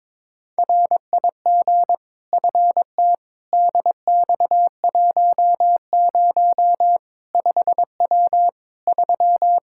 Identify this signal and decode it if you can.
Morse code, heard as a keyed tone, about 22 wpm, 700 Hz. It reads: RIG FT DX10 5W 3
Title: Practice QSO - Exchange 2 - Info - 22wpm